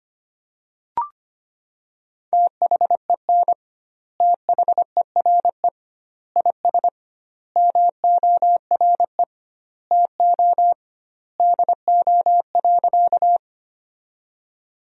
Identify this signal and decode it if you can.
Morse code, heard as a keyed tone, about 25 wpm, 700 Hz. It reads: THEN THERE IS MORE TO DO.
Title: 2x Word-Spacing - Sentences from Top 100 Words - 25wpm